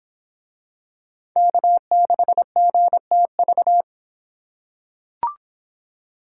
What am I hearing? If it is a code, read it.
Morse code, heard as a keyed tone, about 23 wpm, 700 Hz. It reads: K6GTV